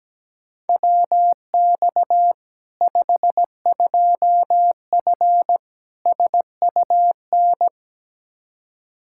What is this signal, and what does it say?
Morse code, heard as a keyed tone, about 17 wpm, 700 Hz. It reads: WX 52F SUN
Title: Practice QSO - Exchange 2 - Info - 17wpm